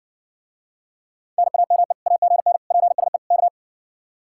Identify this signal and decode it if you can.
Morse code, heard as a keyed tone, about 60 wpm, 700 Hz. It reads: NAME AGN PSE K